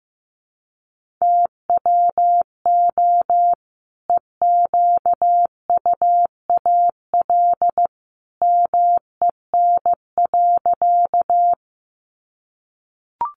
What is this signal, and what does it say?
Morse code, heard as a keyed tone, about 15 wpm, 700 Hz. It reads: TWO EQUAL MEN.
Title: Sentences from Top 1400 Words 15wpm